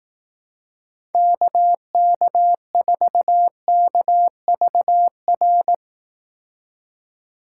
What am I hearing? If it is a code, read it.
Morse code, heard as a keyed tone, about 18 wpm, 700 Hz. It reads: KK4KVR
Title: Speed Racing - USA Call Signs 15wpm